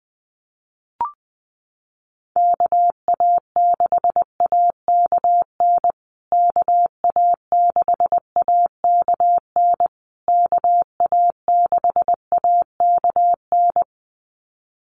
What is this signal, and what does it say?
Morse code, heard as a keyed tone, about 19 wpm, 700 Hz. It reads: KA6AKN KA6AKN KA6AKN
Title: USA Call Signs Repeated 3x 20wpm